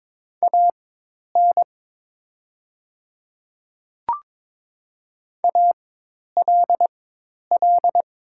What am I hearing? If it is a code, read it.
Morse code, heard as a keyed tone, about 22 wpm, spaced as for 10 wpm, 700 Hz. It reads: AN ALL
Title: 4x Character Spacing - Top 100 Words - 22wpm